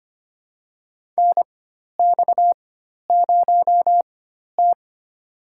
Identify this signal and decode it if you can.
Morse code, heard as a keyed tone, about 25 wpm, spaced as for 12 wpm, 690 Hz. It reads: NX0T
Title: USA Call Signs - 4 Characters - 4x Character Spacing - 25wpm